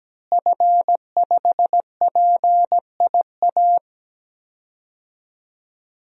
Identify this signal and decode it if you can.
Morse code, heard as a keyed tone, about 17 wpm, 700 Hz. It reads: F5PIA
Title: DXCC-100 Call Signs - 5 Characters - Repeated 3x - 17wpm